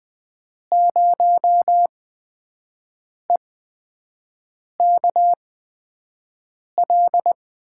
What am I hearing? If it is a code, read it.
Morse code, heard as a keyed tone, about 20 wpm, spaced as for 5 wpm, 700 Hz. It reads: 0EKL
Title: USA Call Signs - 6 Characters - 8x Character Spacing - 20wpm